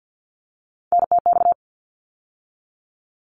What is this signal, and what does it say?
Morse code, heard as a keyed tone, about 50 wpm, 700 Hz. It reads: NTX
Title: ARRL Sections 50wpm